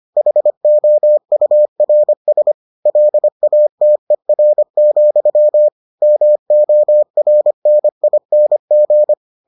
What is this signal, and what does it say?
Morse code, heard as a keyed tone, about 25 wpm, 600 Hz. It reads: HOURS LATER, MORNING